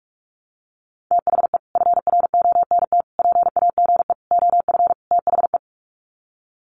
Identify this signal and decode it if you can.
Morse code, heard as a keyed tone, about 44 wpm, 700 Hz. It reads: THE FRONT PAGE OF THE
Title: Sets of 5 Words Encore 45wpm